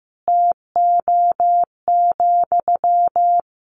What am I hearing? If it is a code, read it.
Morse code, heard as a keyed tone, about 15 wpm, 700 Hz. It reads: TO,
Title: Sentences from Top 200 Words 15wpm